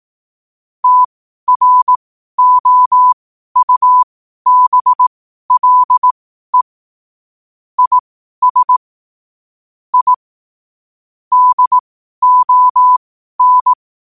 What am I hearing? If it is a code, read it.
Morse code, heard as a keyed tone, about 18 wpm, spaced as for 12 wpm, 1000 Hz. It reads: TROUBLE IS I DON
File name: qotdfiles / 12wpm